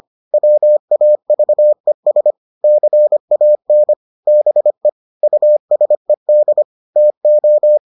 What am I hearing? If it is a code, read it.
Morse code, heard as a keyed tone, about 25 wpm, 600 Hz. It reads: WAVES CAN BE USED TO